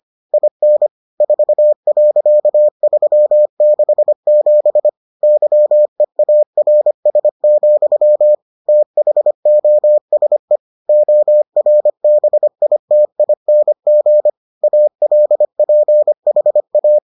Morse code, heard as a keyed tone, about 25 wpm, 600 Hz. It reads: IN 4.367 YEARS, THOSE ORBITING ALPHA